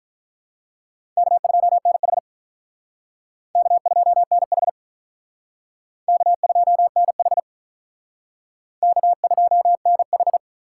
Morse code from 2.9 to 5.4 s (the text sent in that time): K2NH